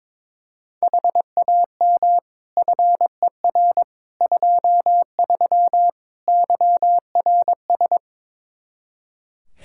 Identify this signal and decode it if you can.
Morse code, heard as a keyed tone, about 22 wpm, 700 Hz. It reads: HAM FER 23 YRS
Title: K1USN SST QSO Practice 22wpm